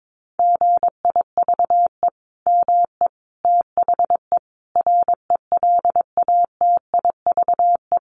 Morse code, heard as a keyed tone, about 22 wpm, 700 Hz. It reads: GIVE ME THE RELATIVE
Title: Sentences from Top 1400 Words 22wpm